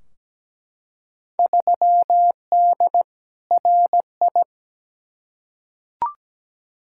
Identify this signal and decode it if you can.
Morse code, heard as a keyed tone, about 17 wpm, 700 Hz. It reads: 3D RI
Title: Field Day Exchanges Encore - 17wpm